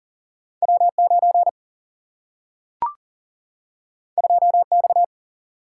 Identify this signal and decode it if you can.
Morse code, heard as a keyed tone, about 40 wpm, 700 Hz. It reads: W 9 2 X
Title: Single-Letter Single-Number 40wpm